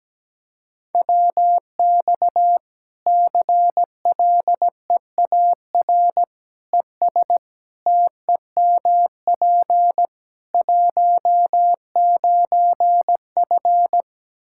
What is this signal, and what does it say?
Morse code, heard as a keyed tone, about 17 wpm, 700 Hz. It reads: WX CLEAR ES TEMP 19F
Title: Practice QSO - Exchange 2 - Info - 17wpm